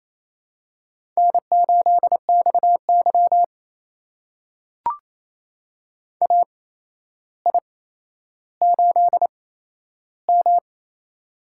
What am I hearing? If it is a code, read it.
Morse code, heard as a keyed tone, about 28 wpm, spaced as for 12 wpm, 700 Hz. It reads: N8XY AI8M
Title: USA Call Signs - 4 Characters - 8x Character Spacing - 28wpm